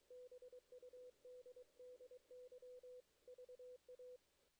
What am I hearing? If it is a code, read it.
Morse code, heard as a keyed tone, about 22 wpm, 500 Hz. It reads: BUDDY VA
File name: 2016 JAN CW NAQP / 2016 Jan naqp k5ot - 0-8